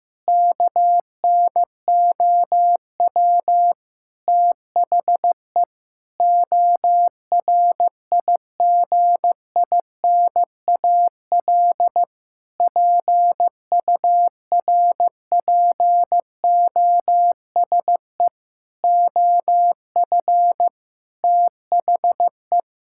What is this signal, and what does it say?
Morse code, heard as a keyed tone, about 16 wpm, 700 Hz. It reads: KNOW THE ORIGINAL PURPOSE OF THE
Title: Sentences from Top 1000 Words - Review Part 6 - 15wpm